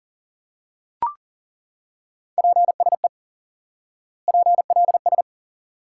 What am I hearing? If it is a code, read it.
Morse code, heard as a keyed tone, about 40 wpm, 700 Hz. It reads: PSE PLS
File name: IQE Course 3 - Lesson 89 - YAGI - Two Elements - 40wpm